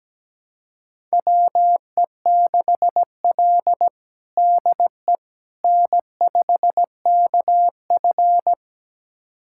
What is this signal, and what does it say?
Morse code, heard as a keyed tone, about 17 wpm, 700 Hz. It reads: WE6L DE N5KF
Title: USA Callsign DE Callsign - 4 Characters - 17wpm